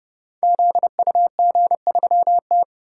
Morse code, heard as a keyed tone, about 30 wpm, 700 Hz. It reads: ZUG3T
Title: Code Groups - 5 Letter:Numbers - 30wpm